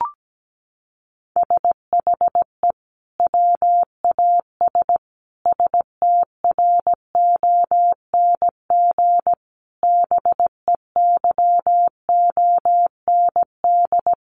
Morse code, heard as a keyed tone, about 17 wpm, 700 Hz. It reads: SHE WAS STRONG BEYOND
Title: Sentences from Top 1300 Words 17wpm